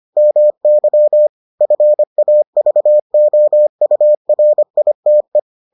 Morse code, heard as a keyed tone, about 25 wpm, 600 Hz. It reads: MY FAVOURITE